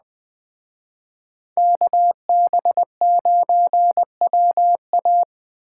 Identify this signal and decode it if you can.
Morse code, heard as a keyed tone, about 20 wpm, 700 Hz. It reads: KB9WA